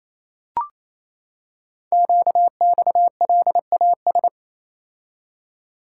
Morse code, heard as a keyed tone, about 28 wpm, 700 Hz. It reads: QXLAS
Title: Code Groups - 5 Letters - 28wpm